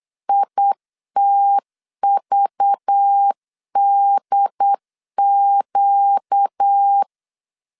Morse code, heard as a keyed tone, about 8 wpm, 800 Hz. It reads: ITVDQ
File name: LEZIONI / 6 PKX0 / 6 PKX 8wpm 3041